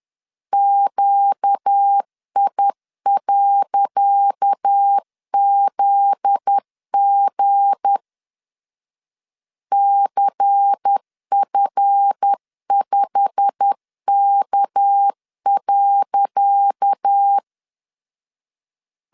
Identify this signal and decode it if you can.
Morse code, heard as a keyed tone, about 10 wpm, 800 Hz. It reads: QI.ZG CF5K.
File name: LEZIONI / 10 SIGNS / 10C DOT 10wpm 2940